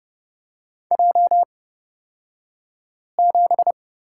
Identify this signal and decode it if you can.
Morse code, heard as a keyed tone, about 29 wpm, spaced as for 5 wpm, 700 Hz. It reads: J7
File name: Lesson 36d - BT - Two Characters 30wpm